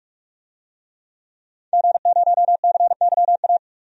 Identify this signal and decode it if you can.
Morse code, heard as a keyed tone, about 41 wpm, 700 Hz. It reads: G0CYA
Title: DXCC-100 Call Signs - 5 Characters - 45wpm